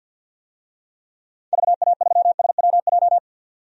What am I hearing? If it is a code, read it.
Morse code, heard as a keyed tone, about 51 wpm, 700 Hz. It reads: VA3SWJ
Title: DXCC-100 Call Signs - 6 Characters - 50wpm